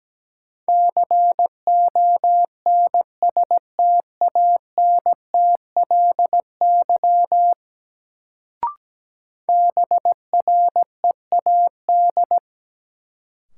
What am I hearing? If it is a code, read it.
Morse code, heard as a keyed tone, about 17 wpm, 700 Hz. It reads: CONSTANTLY BREAD
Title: Sentences from Top 1600 Words - New Word Drill Alt - 17wpm